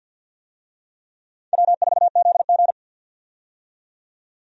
Morse code, heard as a keyed tone, about 52 wpm, 700 Hz. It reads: W4ZG
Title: USA Call Signs - 4 Characters - 50wpm